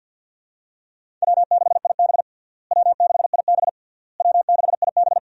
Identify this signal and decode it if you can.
Morse code, heard as a keyed tone, about 51 wpm, 700 Hz. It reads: W6IB W6IB W6IB
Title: USA Call Signs - 4 Characters - Repeated 3x - 50wpm